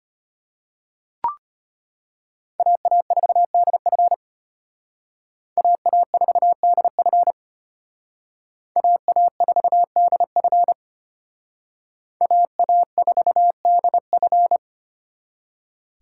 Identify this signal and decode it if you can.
Morse code, heard as a keyed tone, about 32 wpm, 700 Hz. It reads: AA4DF AA4DF AA4DF AA4DF